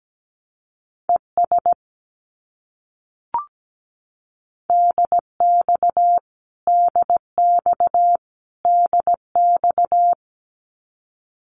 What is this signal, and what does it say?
Morse code, heard as a keyed tone, about 17 wpm, 700 Hz. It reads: ES DX DX DX
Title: Abbreviations Repeated 3x - 17wpm